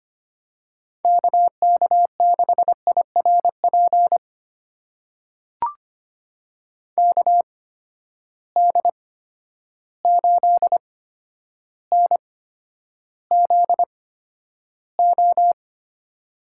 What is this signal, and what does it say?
Morse code, heard as a keyed tone, about 25 wpm, spaced as for 11 wpm, 700 Hz. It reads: KK6IRP KD8NZO
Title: USA Call Signs - 6 Characters - 8x Character Spacing - 25wpm